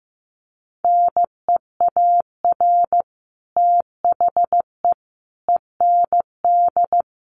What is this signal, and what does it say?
Morse code, heard as a keyed tone, about 15 wpm, 700 Hz. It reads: NEAR THE END
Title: Sets of 5 Words Encore 15wpm